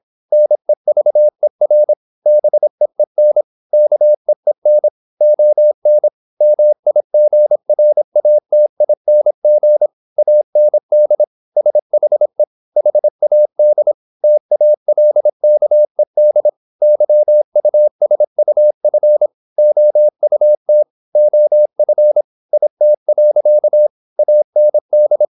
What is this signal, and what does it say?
Morse code, heard as a keyed tone, about 26 wpm, 600 Hz. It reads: NEVER BEEN KEEN ON MIGRATING AND SHE HAD TALKED YUSUF OUT OF IT. AND
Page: Never 26wpm | CW med Gnister